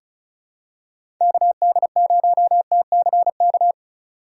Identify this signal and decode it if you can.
Morse code, heard as a keyed tone, about 34 wpm, 700 Hz. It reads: KD0TCK
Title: USA Call Signs - 6 Characters - 2x Character Spacing - 35wpm